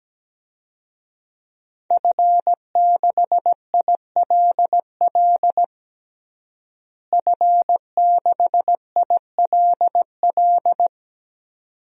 Morse code, heard as a keyed tone, about 17 wpm, 700 Hz. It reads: F6ILL F6ILL
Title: DXCC-100 Call Signs - 5 Characters - Repeated 2x - 17wpm